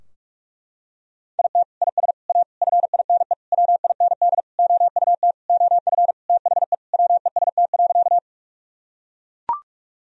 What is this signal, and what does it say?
Morse code, heard as a keyed tone, about 41 wpm, 700 Hz. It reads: IT IS A FINE WIND OUT OF THE WEST.
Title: Sentences from Top 1000 Words - Review Part 2 - 45wpm